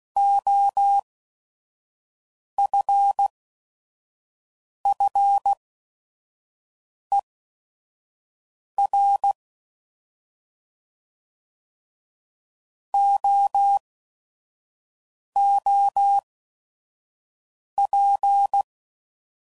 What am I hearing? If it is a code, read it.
Morse code, heard as a keyed tone, about 16 wpm, spaced as for 5 wpm, 800 Hz. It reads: OFFER OOP
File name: Prefixes / By First Letter / Os